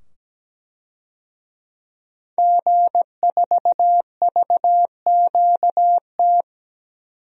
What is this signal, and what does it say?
Morse code, heard as a keyed tone, about 17 wpm, 700 Hz. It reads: G4VQT
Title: DXCC-100 Call Signs - 5 Characters - 17wpm